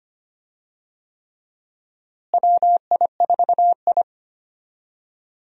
Morse code, heard as a keyed tone, about 25 wpm, 700 Hz. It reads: WI4I
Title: USA Call Signs - 4 Characters - Repeated 3x - 25wpm